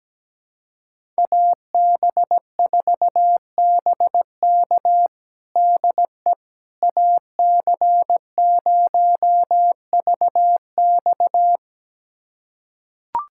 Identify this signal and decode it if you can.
Morse code, heard as a keyed tone, about 17 wpm, 700 Hz. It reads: AB4BK DE AC0VX
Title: USA Callsign DE Callsign - 5 Characters - 17wpm